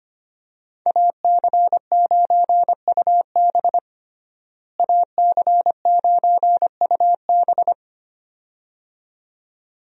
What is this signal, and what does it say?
Morse code, heard as a keyed tone, about 25 wpm, 700 Hz. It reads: AC9UB AC9UB
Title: USA Call Signs - 5 Characters - Repeated 2x - 25wpm